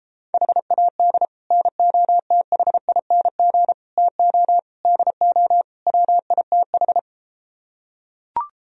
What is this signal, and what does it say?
Morse code, heard as a keyed tone, about 34 wpm, 700 Hz. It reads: HAD NOTHING TO DO WITH